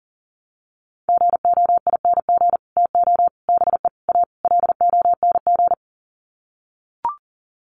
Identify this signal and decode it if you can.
Morse code, heard as a keyed tone, about 40 wpm, 700 Hz. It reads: GOING TO BE A LONG
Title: Sets of 5 Words Encore2 40wpm